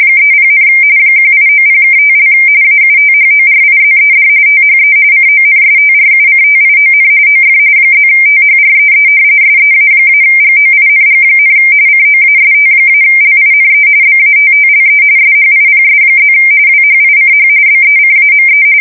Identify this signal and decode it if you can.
RTTY, about 45.45 baud. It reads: 30, 1890(5) AUTHORIZED THE CONSTRUCTION OF +THREE SEA-GOING, COAST-LINE BATTLE SHIPS+ WHICH BECAME
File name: fsk170 / battleships